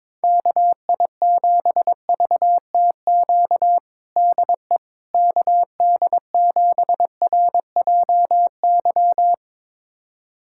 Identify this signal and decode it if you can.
Morse code, heard as a keyed tone, about 22 wpm, 700 Hz. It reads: KI7VTQ DE KD7RJY
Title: USA Callsign DE Callsign - 6 Characters - 22wpm